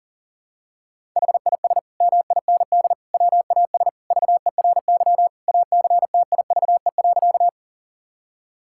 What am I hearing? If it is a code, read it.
Morse code, heard as a keyed tone, about 40 wpm, 700 Hz. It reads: HIS MIND WAS VERY ACTIVE.
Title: Sentences from Top 1000 Words - Review Part 6 - 40wpm